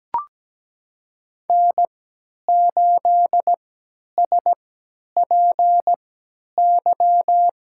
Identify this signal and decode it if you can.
Morse code, heard as a keyed tone, about 17 wpm, spaced as for 10 wpm, 700 Hz. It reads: N8SPY
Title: USA Call Signs - 5 Characters - 3x Character Spacing - 17wpm